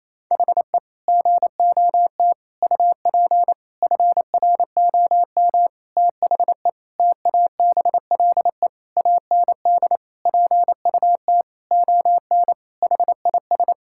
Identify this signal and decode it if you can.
Morse code, heard as a keyed tone, about 28 wpm, 700 Hz. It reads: HE GOT UP FROM THE TABLE AND PUT ON HIS
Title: Sentences from Top 1600 Words 28wpm